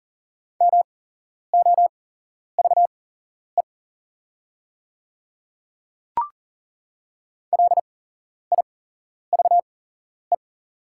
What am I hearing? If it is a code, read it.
Morse code, heard as a keyed tone, about 38 wpm, spaced as for 11 wpm, 690 Hz. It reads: MOVE LIVE
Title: 8x Character Spacing - Top 300 Words - 40wpm